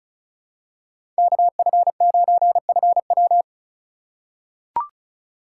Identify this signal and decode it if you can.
Morse code, heard as a keyed tone, about 36 wpm, 700 Hz. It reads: KF9FW
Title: USA Call Signs - 3x Character Spacing - 35wpm